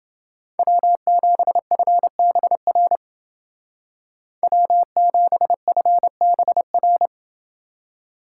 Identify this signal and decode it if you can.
Morse code, heard as a keyed tone, about 28 wpm, 700 Hz. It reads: W7FBR W7FBR